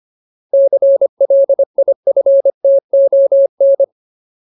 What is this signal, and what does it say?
Morse code, heard as a keyed tone, about 25 wpm, 550 Hz. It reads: CLIFTON
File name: Ultimate Word List / 25WPM Ultimate